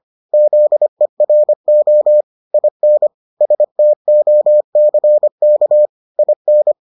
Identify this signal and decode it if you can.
Morse code, heard as a keyed tone, about 25 wpm, 600 Hz. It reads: ZERO IN STOCK IN